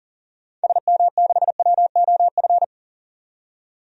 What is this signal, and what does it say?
Morse code, heard as a keyed tone, about 40 wpm, 700 Hz. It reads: SM6WOF
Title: DXCC-100 Call Signs - 6 Characters - 40wpm